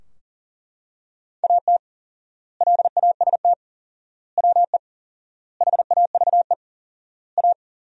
Morse code, heard as a keyed tone, about 40 wpm, 700 Hz. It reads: AT LAST WE HAVE A
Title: 4x Word-Spacing - Sentences from Top 300 Words - 40wpm